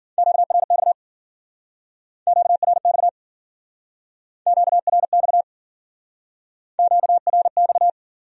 Morse code, heard as a keyed tone, about 52 wpm, 700 Hz. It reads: QRX QRX QRX QRX